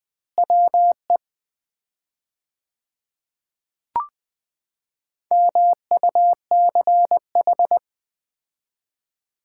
Morse code, heard as a keyed tone, about 20 wpm, 700 Hz. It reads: WE MUCH
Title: 4x Word-Spacing - Sets of 2 Words - 20wpm